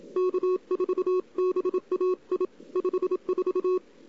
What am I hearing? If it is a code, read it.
Morse code, heard as a keyed tone, about 27 wpm, 370 Hz. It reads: K4BAI 54
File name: CW / 2008-ss-cw-k5tr-op-n5rz -02:00